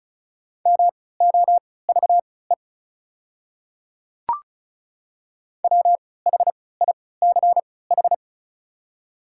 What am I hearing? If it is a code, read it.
Morse code, heard as a keyed tone, about 36 wpm, spaced as for 20 wpm, 700 Hz. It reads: MOVE WHICH
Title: 3x Character Spacing - Top 300 Words - 35wpm